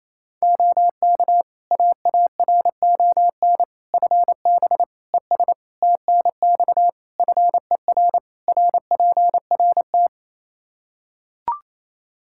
Morse code, heard as a keyed tone, about 27 wpm, 700 Hz. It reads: OK AARON FB ES TNX FER RPRT
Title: K1USN SST QSO Practice 28wpm